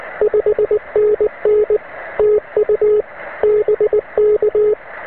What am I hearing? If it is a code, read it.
Morse code, heard as a keyed tone, about 19 wpm, 420 Hz, with radio noise. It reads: 5NN TU BK